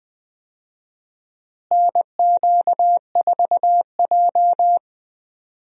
Morse code, heard as a keyed tone, about 20 wpm, 700 Hz. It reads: NQ4J
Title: USA Call Signs - 4 Characters - Repeated 3x - 20wpm